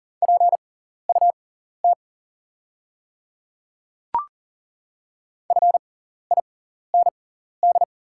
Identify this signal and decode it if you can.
Morse code, heard as a keyed tone, about 37 wpm, spaced as for 14 wpm, 700 Hz. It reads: PUT FIND
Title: 6x Character Spacing - Top 200 Words - 40wpm